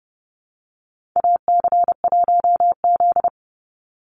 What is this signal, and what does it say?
Morse code, heard as a keyed tone, about 30 wpm, 700 Hz. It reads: AC1Z
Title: NA SOTA Chaser Callsigns 30wpm